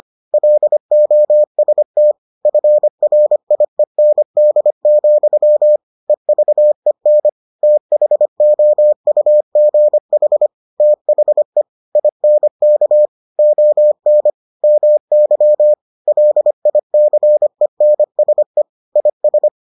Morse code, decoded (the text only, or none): LOST FRIEND, EVEN THOUGH THE INK ON MY LICENSE IS